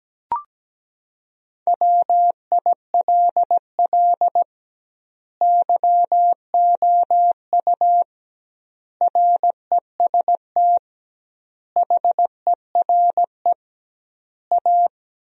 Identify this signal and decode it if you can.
Morse code, heard as a keyed tone, about 17 wpm, 700 Hz. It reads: WILL YOU REST HERE A
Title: 2x Word-Spacing - Sentences from Top 500 Words - 17wpm